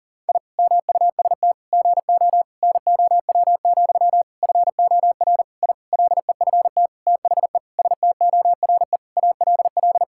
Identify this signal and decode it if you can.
Morse code, heard as a keyed tone, about 40 wpm, 700 Hz. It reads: I MUST GO NOW, FOR I LEFT THE STORE ALL
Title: Sentences from Top 1000 Words - Review Part 5 - 40wpm